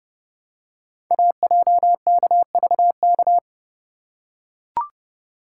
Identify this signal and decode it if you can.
Morse code, heard as a keyed tone, about 30 wpm, 700 Hz. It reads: AJKVK
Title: Code Groups - 5 Letters - 30wpm